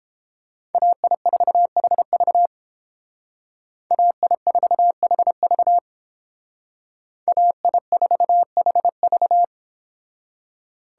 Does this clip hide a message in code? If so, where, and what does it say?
Yes: Morse code from 0.7 to 9.4 s, heard as a keyed tone, about 30 wpm, 700 Hz. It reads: AI4HV AI4HV AI4HV